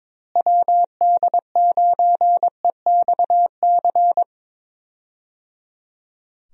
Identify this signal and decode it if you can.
Morse code, heard as a keyed tone, about 22 wpm, 700 Hz. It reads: WD9EXC
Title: USA Call Signs - 6 Characters - Repeated 3x - 22wpm